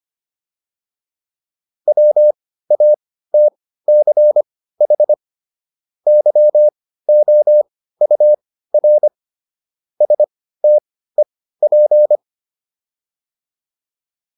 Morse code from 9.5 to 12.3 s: STEP

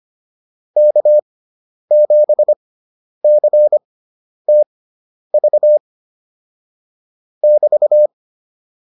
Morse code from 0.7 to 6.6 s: K7CTV

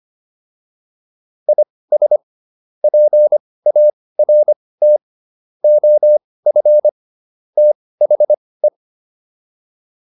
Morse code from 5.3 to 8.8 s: OF THE

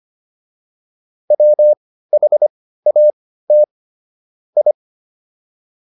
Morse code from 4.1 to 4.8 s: I

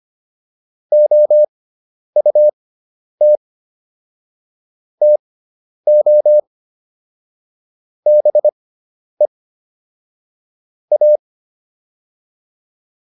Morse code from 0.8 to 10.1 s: OUT TO BE

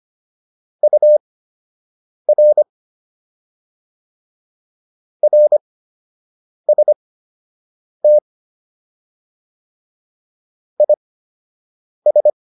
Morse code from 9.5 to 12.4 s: IS

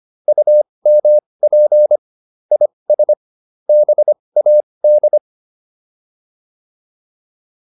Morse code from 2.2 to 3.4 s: IS

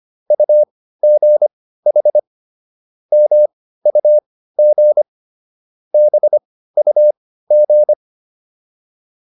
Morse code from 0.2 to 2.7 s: UGH

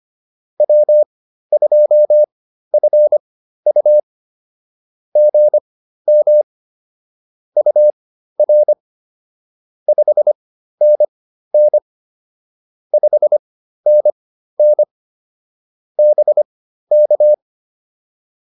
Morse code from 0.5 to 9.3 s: W2FU GM UR

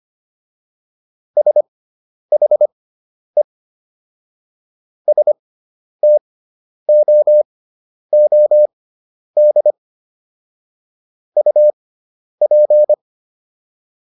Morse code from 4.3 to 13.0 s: STOOD UP

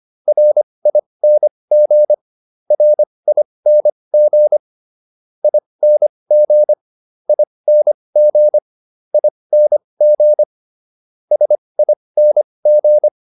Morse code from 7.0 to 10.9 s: ING ING

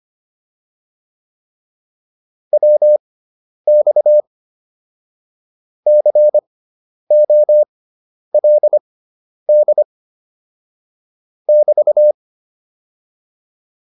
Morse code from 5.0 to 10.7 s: COLD

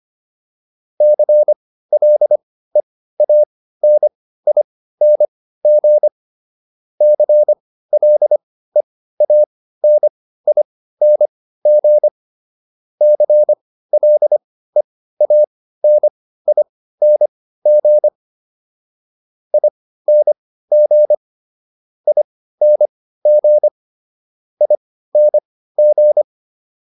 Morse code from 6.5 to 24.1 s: CLEANING CLEANING ING ING

